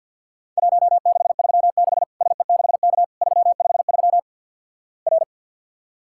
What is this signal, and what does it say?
Morse code, heard as a keyed tone, about 50 wpm, 700 Hz. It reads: 1636 SE6K 353 R
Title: Worked All Europe DX Contest - QTC Exchange - 50wpm